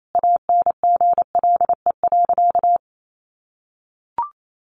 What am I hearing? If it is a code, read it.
Morse code, heard as a keyed tone, about 28 wpm, 700 Hz. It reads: ANGLE.
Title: Sentences from Top 1500 Words 28wpm